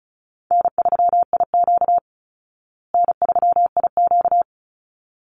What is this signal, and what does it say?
Morse code, heard as a keyed tone, about 34 wpm, 700 Hz. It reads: N3IQ N3IQ
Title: Call Signs Repeated 3x - 35wpm